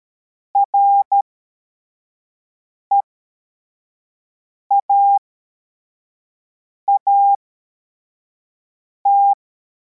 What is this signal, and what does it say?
Morse code, heard as a keyed tone, about 13 wpm, spaced as for 4 wpm, 800 Hz. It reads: REAAT
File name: formation / cw / Text25